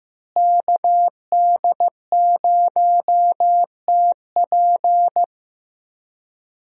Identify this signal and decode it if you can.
Morse code, heard as a keyed tone, about 15 wpm, 700 Hz. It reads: KD0TP
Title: USA Call Signs - 5 Characters - Repeated 3x - 15wpm